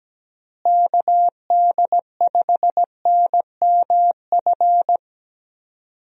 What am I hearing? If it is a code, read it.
Morse code, heard as a keyed tone, about 17 wpm, 700 Hz. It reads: KD5NMF
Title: USA Call Signs - 6 Characters - Repeated 3x - 17wpm